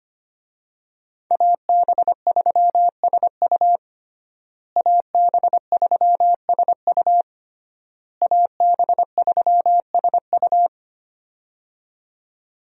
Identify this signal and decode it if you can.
Morse code, heard as a keyed tone, about 25 wpm, 700 Hz. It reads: AB3SU AB3SU AB3SU
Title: USA Call Signs - 5 Characters - Repeated 3x - 25wpm